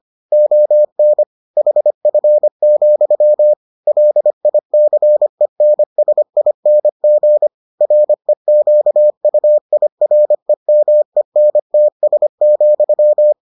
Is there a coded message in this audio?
Yes: Morse code from 0.3 to 13.3 s, heard as a keyed tone, about 25 wpm, 600 Hz. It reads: ON HF, LICENSING REQUIREMENTS,